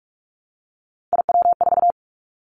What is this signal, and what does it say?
Morse code, heard as a keyed tone, about 46 wpm, 700 Hz. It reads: IW4
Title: Callsign Prefixes for Top 50 DXCC - 45wpm